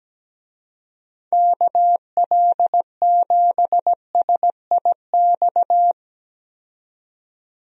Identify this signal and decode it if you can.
Morse code, heard as a keyed tone, about 17 wpm, 700 Hz. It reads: KL7SIX
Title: DXCC-100 Call Signs - 6 Characters - 17wpm